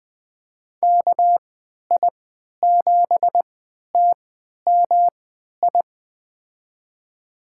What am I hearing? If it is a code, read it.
Morse code, heard as a keyed tone, about 20 wpm, spaced as for 11 wpm, 700 Hz. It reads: KI7TMI
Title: USA Call Signs - 6 Characters - 3x Character Spacing - 20wpm